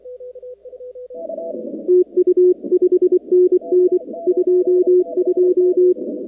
Morse code, heard as a keyed tone, about 24 wpm, 360 Hz. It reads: TU 5NN 22
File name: contests / 2019-cw-wpx-kt5j-k5pi- -2330 0-46